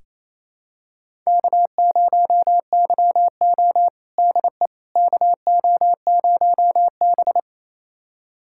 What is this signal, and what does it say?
Morse code, heard as a keyed tone, about 28 wpm, 700 Hz. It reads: K0YO DE KO0B
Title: USA Callsign DE Callsign - 4 Characters - 28wpm